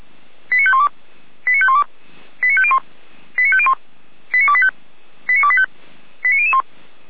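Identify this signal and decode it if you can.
Five-tone selective call (ZVEI), heard as quick runs of five tones, tones of about 70 ms. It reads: 89512 89512 89501 89501 89285 89285 89001
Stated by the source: ZVEI five-tone selective call received by SDR.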